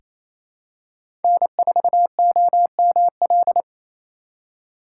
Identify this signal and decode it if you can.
Morse code, heard as a keyed tone, about 28 wpm, 700 Hz. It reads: N4OML
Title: USA Call Signs - 5 Characters - 2x Character Spacing - 28wpm